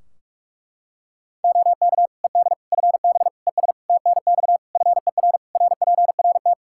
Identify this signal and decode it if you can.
Morse code, heard as a keyed tone, about 46 wpm, 700 Hz. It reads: OK ED FB ES TNX FER RPRT
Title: K1USN SST QSO Practice 45wpm